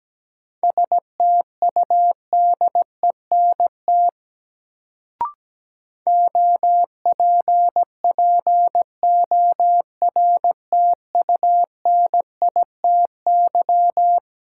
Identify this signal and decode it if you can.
Morse code, heard as a keyed tone, about 17 wpm, 700 Hz. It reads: STUDENT OPPORTUNITY
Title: Sentences from Top 1000 Words - New Word Drill - 17wpm